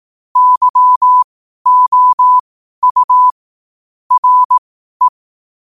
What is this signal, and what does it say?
Morse code, heard as a keyed tone, about 18 wpm, spaced as for 12 wpm, 1000 Hz. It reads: YOU RE
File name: qotdfiles / 12wpm